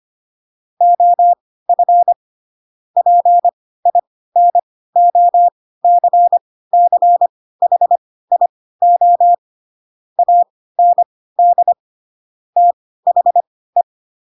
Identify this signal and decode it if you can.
Morse code, heard as a keyed tone, about 25 wpm, spaced as for 16 wpm, 700 Hz. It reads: OF PINOCCHIO AND THE
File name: Pinocchio / Pinocchio ch4 10 25 ik1hge